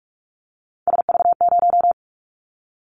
Morse code, heard as a keyed tone, about 44 wpm, 700 Hz. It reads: SV0
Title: Callsign Prefixes for Top 50 DXCC - 45wpm